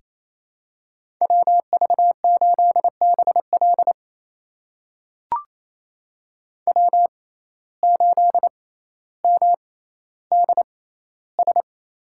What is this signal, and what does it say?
Morse code, heard as a keyed tone, about 27 wpm, spaced as for 14 wpm, 700 Hz. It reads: WV8BL W8MDS
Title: USA Call Signs - 5 Characters - 6x Character Spacing - 28wpm